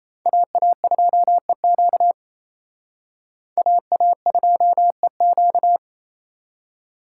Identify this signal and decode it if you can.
Morse code, heard as a keyed tone, about 35 wpm, 700 Hz. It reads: AA2EQ AA2EQ